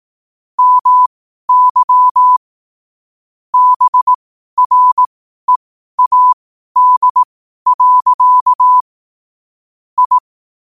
Morse code, heard as a keyed tone, about 18 wpm, spaced as for 12 wpm, 1000 Hz. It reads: MY BREAD. I
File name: qotdfiles / 12wpm